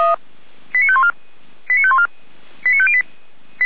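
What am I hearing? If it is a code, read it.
Five-tone selective call (ZVEI), heard as quick runs of five tones, tones of about 70 ms. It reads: 89514 89514 89508
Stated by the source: Five-tone ZVEI selective call 89337 received on SDR, from the server of feuerwehr-orlamuende (fire brigade Orlamünde).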